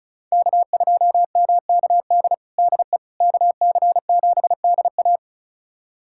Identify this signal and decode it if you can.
Morse code, heard as a keyed tone, about 36 wpm, 700 Hz. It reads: K2MKD DE KC7DA
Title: USA Callsign DE Callsign - 5 Characters - 35wpm